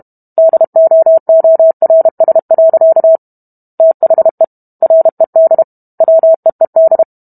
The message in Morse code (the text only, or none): DOORS. THE RED WEED